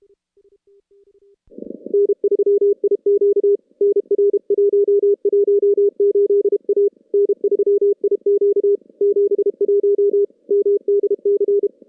Morse code, heard as a keyed tone, about 30 wpm, 400 Hz. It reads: N3IQ NR118A N3IQ 71 MDC
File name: contests / CW-SS-k5tr- -2300